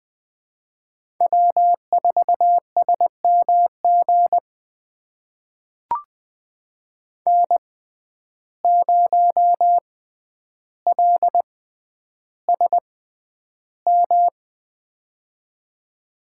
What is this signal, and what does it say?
Morse code, heard as a keyed tone, about 21 wpm, spaced as for 10 wpm, 700 Hz. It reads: W4SMG N0LSM
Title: USA Call Signs - 5 Characters - 6x Character Spacing - 20wpm